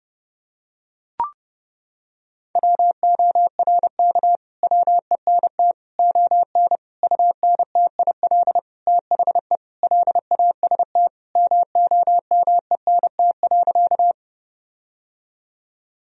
Morse code, heard as a keyed tone, about 30 wpm, 700 Hz. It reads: WORK WENT ON UNTIL THE LAST MOMENT.
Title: Sentences from Top 1000 Words - Review Part 1 - 30wpm